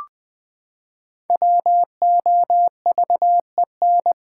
Morse code, heard as a keyed tone, about 20 wpm, 700 Hz. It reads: WOVEN
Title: Pangram Challenge v2 - 20wpm